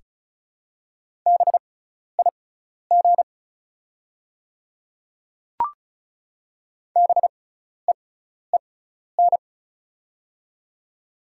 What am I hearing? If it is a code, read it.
Morse code, heard as a keyed tone, about 34 wpm, spaced as for 12 wpm, 690 Hz. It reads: BIG BEEN
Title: 6x Character Spacing - Top 500 Words - 35wpm